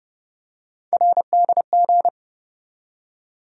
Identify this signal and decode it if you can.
Morse code, heard as a keyed tone, about 32 wpm, 700 Hz. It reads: RDG
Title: USA Call Sign Postfixes - 3 Letters 30wpm